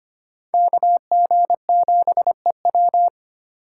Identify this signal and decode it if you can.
Morse code, heard as a keyed tone, about 32 wpm, 700 Hz. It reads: KG7EW